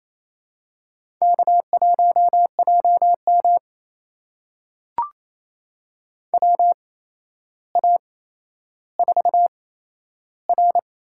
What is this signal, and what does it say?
Morse code, heard as a keyed tone, about 28 wpm, spaced as for 12 wpm, 700 Hz. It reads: K1JM WA4R
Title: USA Call Signs - 8x Character Spacing - 28wpm